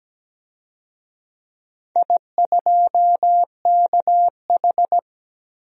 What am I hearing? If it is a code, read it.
Morse code, heard as a keyed tone, about 17 wpm, 700 Hz. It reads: I2KH
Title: DXCC-100 Call Signs - 4 Characters - 17wpm